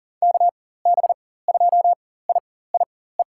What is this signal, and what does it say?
Morse code, heard as a keyed tone, about 40 wpm, spaced as for 19 wpm, 700 Hz. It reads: KB2IIE
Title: USA Call Signs - 6 Characters - 4x Character Spacing - 40wpm